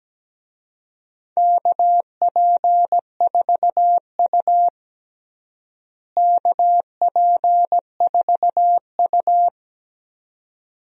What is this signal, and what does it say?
Morse code, heard as a keyed tone, about 17 wpm, 700 Hz. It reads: KP4U KP4U
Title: DXCC-100 Call Signs - 4 Characters - Repeated 2x - 17wpm